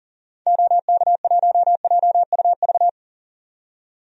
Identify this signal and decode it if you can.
Morse code, heard as a keyed tone, about 40 wpm, 700 Hz. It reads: OK1JUV
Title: DXCC-100 Call Signs - 6 Characters - 40wpm